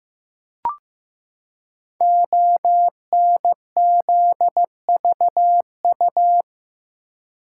Morse code, heard as a keyed tone, about 15 wpm, 700 Hz. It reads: ONZVU
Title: Code Groups - 5 Letter:Numbers - 15wpm